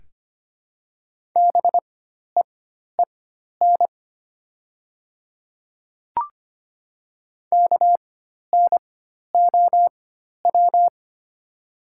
Morse code, heard as a keyed tone, about 25 wpm, spaced as for 12 wpm, 700 Hz. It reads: BEEN KNOW
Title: 4x Character Spacing - Top 100 Words - 25wpm